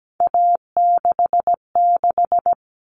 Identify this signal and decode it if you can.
Morse code, heard as a keyed tone, about 17 wpm, 700 Hz. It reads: A66
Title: All Cut Numbers - 3-Digit 17wpm